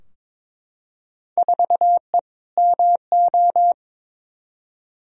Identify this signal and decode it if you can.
Morse code, heard as a keyed tone, about 22 wpm, 700 Hz. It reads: 4E MO
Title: Field Day Exchanges Encore - 22wpm